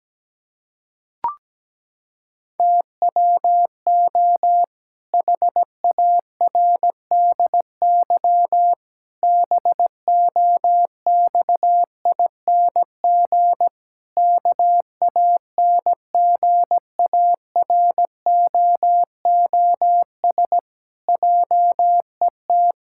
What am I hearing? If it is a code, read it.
Morse code, heard as a keyed tone, about 17 wpm, 700 Hz. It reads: TWO HARDY BOXING KANGAROOS JET
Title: Pangram Challenge v2 - 17wpm